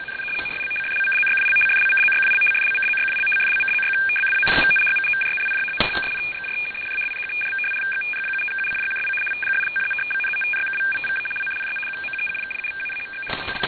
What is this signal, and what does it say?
Two-tone teleprinter signal, about 850 Hz shift.